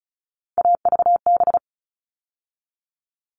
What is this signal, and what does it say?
Morse code, heard as a keyed tone, about 34 wpm, 700 Hz. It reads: AVB
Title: All Cut Numbers - 3-Digit 35wpm